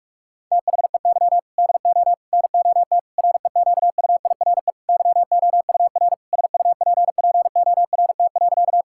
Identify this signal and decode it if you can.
Morse code, heard as a keyed tone, about 46 wpm, 700 Hz. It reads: THEY DO NOT REQUIRE YOUR SUPPORT.
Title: Sentences from Top 1000 Words - Review Part 6 - 45wpm